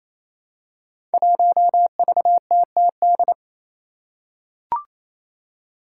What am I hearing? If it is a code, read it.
Morse code, heard as a keyed tone, about 28 wpm, 700 Hz. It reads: 1VTTD
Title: Five Challenging Letter-Number Combinations 28wpm